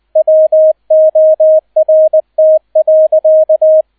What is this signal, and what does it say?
Morse code, heard as a keyed tone, about 20 wpm, 630 Hz, with radio noise. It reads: WORT.